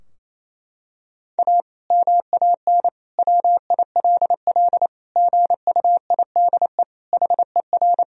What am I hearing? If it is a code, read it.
Morse code, heard as a keyed tone, about 28 wpm, 700 Hz. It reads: A MAN WILL GUIDE HER
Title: Sentences from Top 1000 Words - Review Part 3 - 28wpm